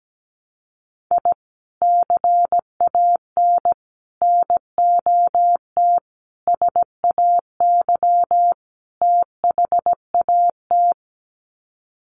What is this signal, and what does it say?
Morse code, heard as a keyed tone, about 17 wpm, 700 Hz. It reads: I CAN NOT SAY THAT
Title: Sets of 5 Words Encore2 17wpm